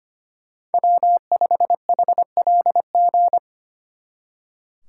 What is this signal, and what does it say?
Morse code, heard as a keyed tone, about 30 wpm, 700 Hz. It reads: W5HLG
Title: Speed Racing - USA 1x3 Call Signs 25wpm